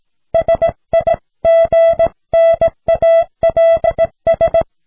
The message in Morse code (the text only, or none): SIGNALS